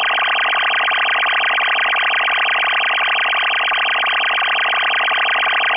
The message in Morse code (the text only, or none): none